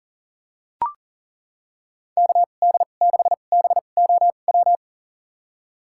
Morse code, heard as a keyed tone, about 40 wpm, spaced as for 29 wpm, 700 Hz. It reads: KD6BOW
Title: USA Call Signs - 6 Characters - 2x Character Spacing - 40wpm